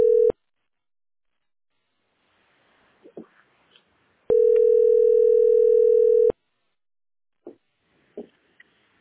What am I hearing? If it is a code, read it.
Ringback tone (North America), 440 + 480 Hz.